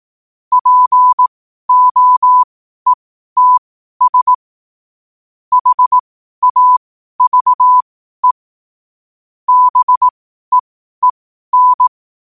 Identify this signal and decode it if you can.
Morse code, heard as a keyed tone, about 18 wpm, spaced as for 12 wpm, 1000 Hz. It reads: POETS HAVE BEEN